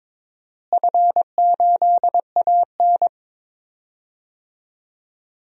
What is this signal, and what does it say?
Morse code, heard as a keyed tone, about 21 wpm, 700 Hz. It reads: F8AN
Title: DXCC-100 Call Signs - 4 Characters - Repeated 2x - 22wpm